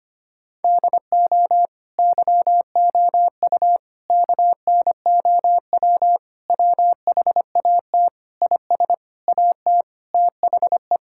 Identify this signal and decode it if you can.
Morse code, heard as a keyed tone, about 25 wpm, 700 Hz. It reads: DO YOU KNOW WHAT IS AT THE
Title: Sentences from Top 1000 Words - Review Part 6 - 25wpm